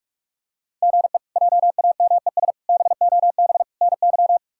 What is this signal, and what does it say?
Morse code, heard as a keyed tone, about 46 wpm, 700 Hz. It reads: GE JAMES BOB NY
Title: K1USN SST QSO Practice - 45wpm